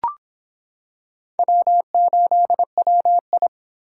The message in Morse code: W8WI

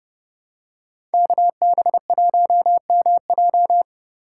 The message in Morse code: KB1MJ

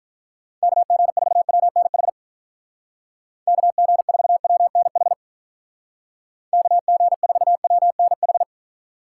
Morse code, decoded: KG4WNH KG4WNH KG4WNH

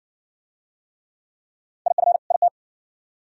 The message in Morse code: IF IT

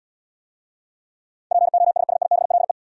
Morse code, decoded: POSSIBLE

